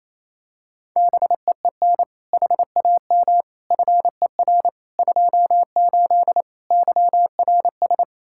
Morse code, heard as a keyed tone, about 27 wpm, 700 Hz. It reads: BEEN HAM FER 28 YRS
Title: Practice QSO - Exchange 3 - More Info - 28wpm